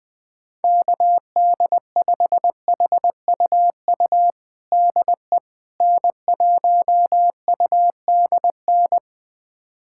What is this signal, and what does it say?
Morse code, heard as a keyed tone, about 20 wpm, 700 Hz. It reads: KD5HUU DE N1UDN
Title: USA Callsign DE Callsign 20wpm